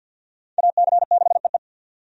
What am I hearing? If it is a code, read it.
Morse code, heard as a keyed tone, about 46 wpm, 700 Hz. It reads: AC6EE
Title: USA Call Signs - 5 Characters - 50wpm